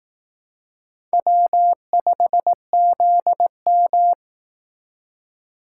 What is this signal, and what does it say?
Morse code, heard as a keyed tone, about 19 wpm, 700 Hz. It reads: W5ZM